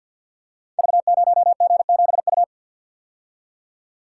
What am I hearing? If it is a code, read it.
Morse code, heard as a keyed tone, about 53 wpm, 700 Hz. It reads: V0G7U
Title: Code Groups - 5 Letter:Numbers - 50wpm